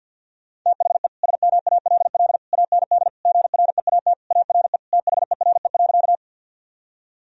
Morse code, heard as a keyed tone, about 51 wpm, 700 Hz. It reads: THE SMALL AND GREAT ARE THERE.
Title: Sentences from Top 1000 Words - Review Part 1 - 50wpm